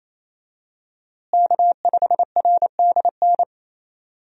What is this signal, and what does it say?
Morse code, heard as a keyed tone, about 28 wpm, 700 Hz. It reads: K5RDN
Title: USA Call Signs - 5 Characters - Repeated 2x - 28wpm